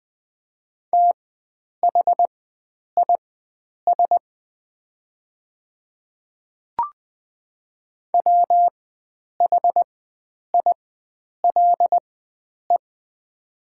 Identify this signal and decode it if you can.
Morse code, heard as a keyed tone, about 20 wpm, spaced as for 9 wpm, 700 Hz. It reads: THIS WHILE
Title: 4x Character Spacing - Top 300 Words - 20wpm